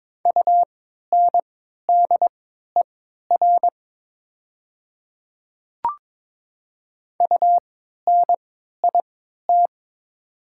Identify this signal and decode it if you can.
Morse code, heard as a keyed tone, about 21 wpm, spaced as for 12 wpm, 700 Hz. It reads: UNDER UNIT
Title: 3x Character Spacing - Top 500 Words - 22wpm